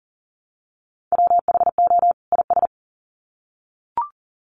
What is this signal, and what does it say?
Morse code, heard as a keyed tone, about 40 wpm, 700 Hz. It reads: WHO IS
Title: Sets of 2 Words 40wpm